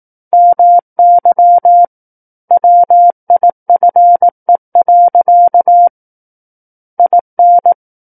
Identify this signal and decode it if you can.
Morse code, heard as a keyed tone, about 18 wpm, 700 Hz. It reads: MY WIFE. IN
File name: 18 WPM